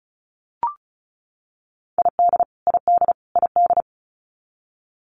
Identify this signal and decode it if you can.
Morse code, heard as a keyed tone, about 34 wpm, 700 Hz. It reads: ID ID ID
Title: US States Repeated 3x 35wpm